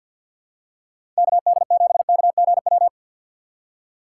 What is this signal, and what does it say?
Morse code, heard as a keyed tone, about 51 wpm, 700 Hz. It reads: KD7KGW
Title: USA Call Signs - 6 Characters - 6x Character Spacing - 50wpm